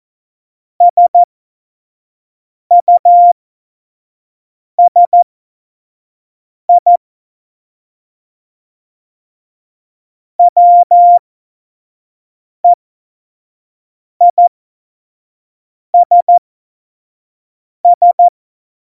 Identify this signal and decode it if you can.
Morse code, heard as a keyed tone, about 14 wpm, spaced as for 5 wpm, 700 Hz. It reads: SUSI WEISS
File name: Morsekurs / Witze / Witz 14. Mai 2019, 25 F70